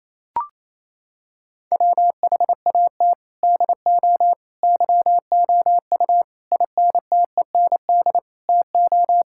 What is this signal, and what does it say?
Morse code, heard as a keyed tone, about 28 wpm, 700 Hz. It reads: WHAT DO YOU INTEND TO
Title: Sentences from Top 1600 Words 28wpm